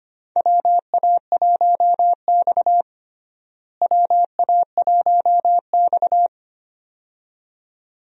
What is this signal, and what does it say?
Morse code, heard as a keyed tone, about 25 wpm, 700 Hz. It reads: WA1X WA1X
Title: USA Call Signs - 4 Characters - Repeated 2x - 25wpm